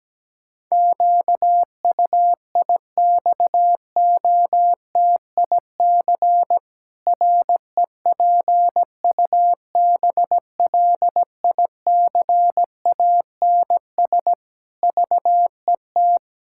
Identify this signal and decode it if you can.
Morse code, heard as a keyed tone, about 17 wpm, 700 Hz. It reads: QUIXOTIC REPUBLICANS VET
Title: Pangram Challenge v2 - 17wpm